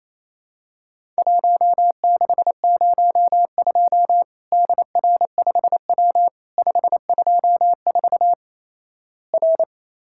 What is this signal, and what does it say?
Morse code, heard as a keyed tone, about 28 wpm, 700 Hz. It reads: 1602 DR5W 524 R
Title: Worked All Europe DX Contest - QTC Exchange - 28wpm